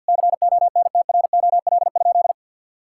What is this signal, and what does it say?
Morse code, heard as a keyed tone, about 51 wpm, 700 Hz. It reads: CONTROL?
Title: Sentences from Top 1000 Words - Review Part 5 - 50wpm